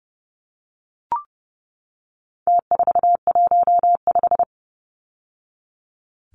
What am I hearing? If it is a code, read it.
Morse code, heard as a keyed tone, about 30 wpm, 700 Hz. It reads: T415
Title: Numbers - Contest Style - 4-Digit 30wpm